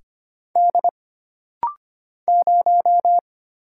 Morse code, heard as a keyed tone, about 25 wpm, 700 Hz. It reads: D 0
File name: Single Character (Letter:Number:Punctuation:Prosign) - Rapid-Fire - 25wpm